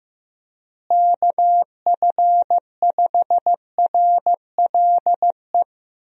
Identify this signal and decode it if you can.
Morse code, heard as a keyed tone, about 15 wpm, 700 Hz. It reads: KF5RLE
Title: USA Call Signs - 6 Characters - Repeated 3x - 15wpm